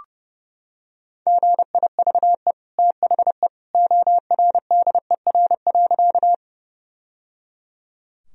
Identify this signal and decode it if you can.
Morse code, heard as a keyed tone, about 30 wpm, 700 Hz. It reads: GIVE THE ORDER.
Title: Sentences from Top 1000 Words - Review Part 2 - 30wpm